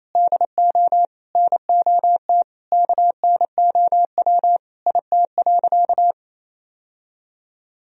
Morse code, heard as a keyed tone, about 28 wpm, 700 Hz. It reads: DO NOT KNOW IT.
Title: Sentences from Top 1000 Words - Review Part 1 - 28wpm